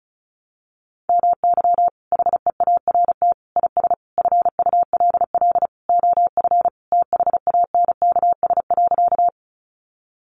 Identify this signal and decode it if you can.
Morse code, heard as a keyed tone, about 34 wpm, 700 Hz. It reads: MY HEART IS FULL OF THANKS.
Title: Sentences from Top 1200 Words 35wpm